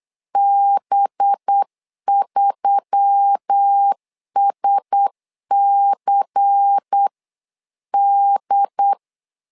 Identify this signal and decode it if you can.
Morse code, heard as a keyed tone, about 8 wpm, 800 Hz. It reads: B3SCD
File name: LEZIONI / 5 LBVC / 5 LBVC 8wpm 2935